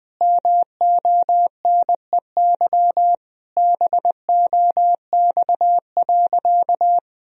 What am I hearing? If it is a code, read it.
Morse code, heard as a keyed tone, about 20 wpm, 700 Hz. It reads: MONEY BOX.
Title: Sentences from Top 1000 Words - Review Part 2 - 20wpm